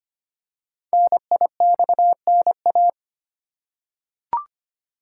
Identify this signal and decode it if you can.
Morse code, heard as a keyed tone, about 25 wpm, 700 Hz. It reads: NIXNA
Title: Code Groups - 5 Letters - 25wpm